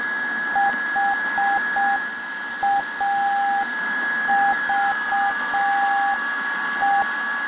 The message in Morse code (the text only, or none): HAVE